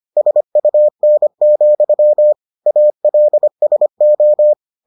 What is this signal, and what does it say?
Morse code, heard as a keyed tone, about 25 wpm, 600 Hz. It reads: SUN, ALSO